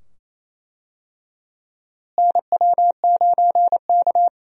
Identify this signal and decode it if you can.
Morse code, heard as a keyed tone, about 27 wpm, 700 Hz. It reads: NW9K
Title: USA Call Signs - 4 Characters - Repeated 2x - 28wpm